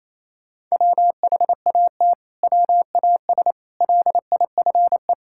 Morse code, heard as a keyed tone, about 28 wpm, 700 Hz. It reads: WHAT WAS LIFE
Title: Sentences from Top 1000 Words - Review Part 5 - 28wpm